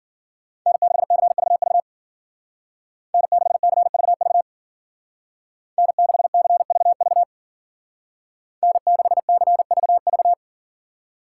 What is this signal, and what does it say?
Morse code, heard as a keyed tone, about 48 wpm, 700 Hz. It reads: N6CVV N6CVV N6CVV N6CVV
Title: Speed Racing - USA Call Signs 40wpm